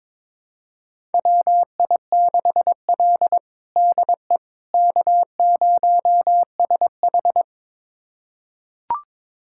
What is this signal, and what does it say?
Morse code, heard as a keyed tone, about 21 wpm, 700 Hz. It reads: WI6L DE K0SH
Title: USA Callsign DE Callsign - 4 Characters - 22wpm